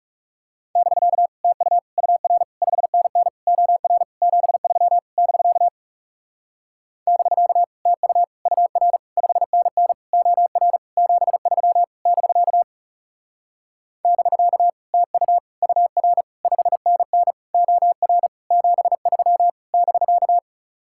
Morse code, heard as a keyed tone, about 40 wpm, 700 Hz. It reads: <BK> TU UR 5NN OR 73 <BK> <BK> TU UR 5NN OR 73 <BK> <BK> TU UR 5NN OR 73 <BK>